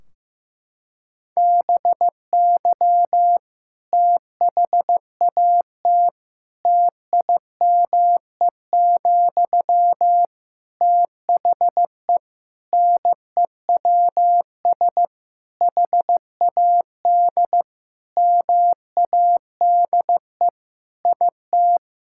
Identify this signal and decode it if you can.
Morse code, heard as a keyed tone, about 15 wpm, 700 Hz. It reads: BY THAT TIME, THE NEWS HAD MADE IT
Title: Sentences from Top 1600 Words 15wpm